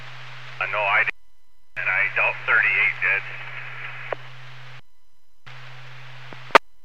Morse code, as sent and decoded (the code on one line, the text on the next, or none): none